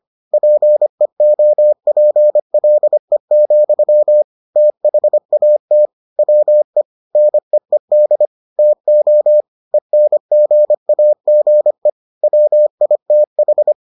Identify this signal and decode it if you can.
Morse code, heard as a keyed tone, about 25 wpm, 600 Hz. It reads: PEOPLE, THAT WE NEED TO ENGAGE WITH